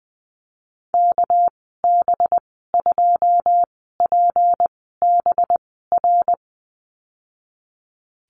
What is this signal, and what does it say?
Morse code, heard as a keyed tone, about 20 wpm, spaced as for 14 wpm, 700 Hz. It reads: KB2PBR
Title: 238 Lesson 38g - Letter q - Call Signs 20wpm